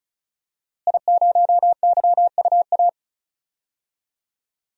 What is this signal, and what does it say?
Morse code, heard as a keyed tone, about 36 wpm, 700 Hz. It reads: I0YUA
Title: DXCC-100 Call Signs - 5 Characters - Repeated 3x - 35wpm